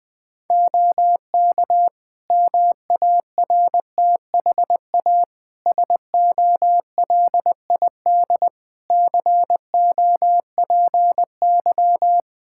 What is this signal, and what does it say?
Morse code, heard as a keyed tone, about 21 wpm, 700 Hz. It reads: OK MARTHA SOLID COPY
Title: Practice QSO - Exchange 3 - More Info - 20wpm